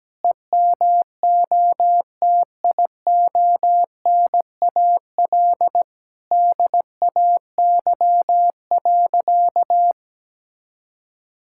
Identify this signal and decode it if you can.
Morse code, heard as a keyed tone, about 17 wpm, 700 Hz. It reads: EMOTIONAL DAY.
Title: Sentences from Top 1000 Words - Review Part 5 - 17wpm